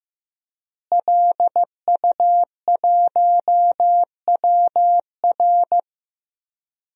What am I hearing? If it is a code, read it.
Morse code, heard as a keyed tone, about 15 wpm, 700 Hz. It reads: LU1WR
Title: DXCC-100 Call Signs - 5 Characters - Repeated 3x - 15wpm